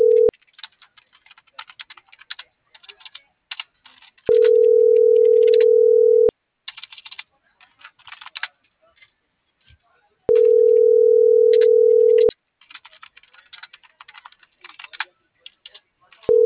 Ringback tone (North America), 440 + 480 Hz.